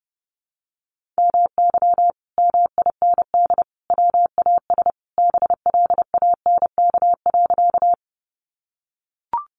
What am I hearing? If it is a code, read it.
Morse code, heard as a keyed tone, about 28 wpm, 700 Hz. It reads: MY MIND WAS BLANK.
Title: Sentences from Top 1500 Words 30wpm